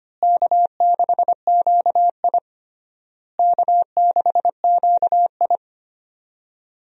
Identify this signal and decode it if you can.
Morse code, heard as a keyed tone, about 25 wpm, 700 Hz. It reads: K6QI K6QI
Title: USA Call Signs - 4 Characters - Repeated 3x - 25wpm